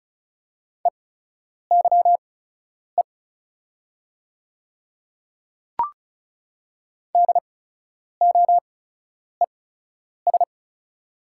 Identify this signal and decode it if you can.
Morse code, heard as a keyed tone, about 36 wpm, spaced as for 10 wpm, 690 Hz. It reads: EYE DOES
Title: 8x Character Spacing - Top 300 Words - 35wpm